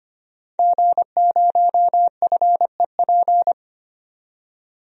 Morse code, heard as a keyed tone, about 25 wpm, 700 Hz. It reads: G0FEP
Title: DXCC-100 Call Signs - 5 Characters - 25wpm